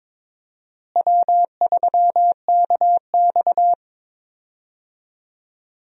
Morse code, heard as a keyed tone, about 22 wpm, 700 Hz. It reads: W3KX
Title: USA Call Signs - 4 Characters - Repeated 2x - 22wpm